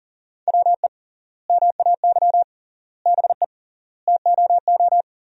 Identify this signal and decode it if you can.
Morse code, heard as a keyed tone, about 40 wpm, 700 Hz. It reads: WE MAY BE TOO
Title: 3x Word-Spacing - Sentences from Top 300 Words - 40wpm